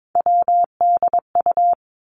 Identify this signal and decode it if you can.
Morse code, heard as a keyed tone, about 22 wpm, 700 Hz. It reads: WDU
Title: Three Challenging Letter-Number Combinations 22wpm